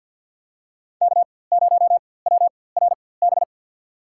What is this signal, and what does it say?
Morse code, heard as a keyed tone, about 51 wpm, spaced as for 23 wpm, 700 Hz. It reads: K0WRB
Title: USA Call Signs - 5 Characters - 4x Character Spacing - 50wpm